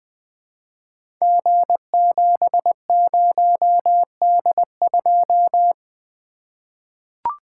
Morse code, heard as a keyed tone, about 20 wpm, 700 Hz. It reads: G70D2
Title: Code Groups - 5 Letter:Numbers - 20wpm